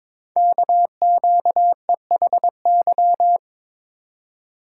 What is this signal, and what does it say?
Morse code, heard as a keyed tone, about 23 wpm, 700 Hz. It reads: KQEHY
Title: Code Groups - 5 Letters - 22wpm